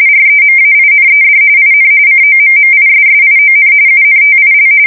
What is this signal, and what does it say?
RTTY, about 45.45 baud. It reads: REMAINED IN COMMISSION UNTIL
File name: fsk170 / battleshipnj